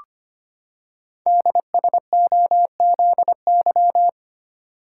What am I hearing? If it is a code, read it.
Morse code, heard as a keyed tone, about 25 wpm, 700 Hz. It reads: DSOZY
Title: Code Groups - 5 Letters - 25wpm